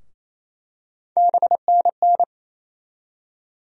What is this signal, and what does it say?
Morse code, heard as a keyed tone, about 28 wpm, 700 Hz. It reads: BNN
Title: USA Call Sign Postfixes - 3 Letters 28wpm